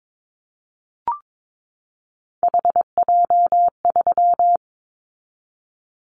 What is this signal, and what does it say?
Morse code, heard as a keyed tone, about 22 wpm, 700 Hz. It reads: HJ3
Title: Callsign Prefixes for Top 100 DXCC - 22wpm